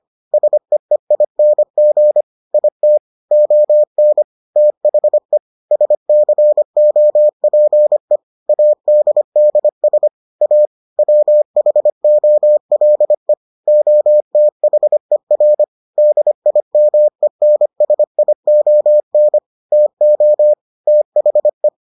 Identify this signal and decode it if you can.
Morse code, heard as a keyed tone, about 25 wpm, 600 Hz. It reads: SEEING IT ON THE SCOPE ADDS A WHOLE OTHER DIMENSION TO THE